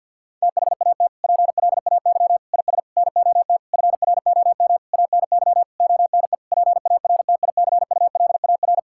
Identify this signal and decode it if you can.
Morse code, heard as a keyed tone, about 50 wpm, 700 Hz. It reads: THAT PLAY IS NOT FROM ANY ONE PARTICULAR
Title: Sentences from Top 1000 Words - Review Part 5 - 50wpm